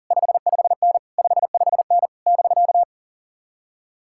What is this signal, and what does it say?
Morse code, heard as a keyed tone, about 52 wpm, 700 Hz. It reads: 55N 55N <BK>